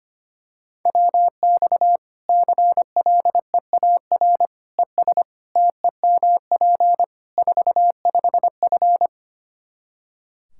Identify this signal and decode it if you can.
Morse code, heard as a keyed tone, about 25 wpm, 700 Hz. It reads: WX CLEAR ES TEMP 45F
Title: Practice QSO - Exchange 2 - Info - 25wpm